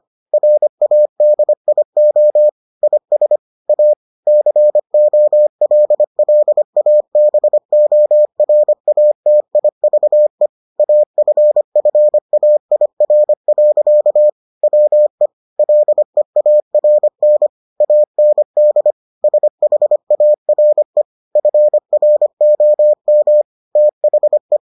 Morse code, heard as a keyed tone, about 25 wpm, 600 Hz. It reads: RADIO IS A COLLABORATIVE AFFAIR. WE LEARN AND SHARE FROM THE